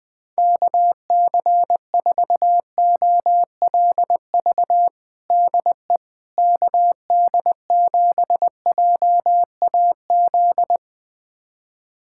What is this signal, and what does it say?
Morse code, heard as a keyed tone, about 20 wpm, 700 Hz. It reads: KC4OLV DE KD7JAZ
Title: USA Callsign DE Callsign - 6 Characters - 20wpm